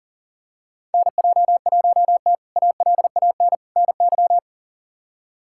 Morse code, heard as a keyed tone, about 40 wpm, 700 Hz. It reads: NJ1T ALAN NY
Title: K1USN SST QSO Practice - 40wpm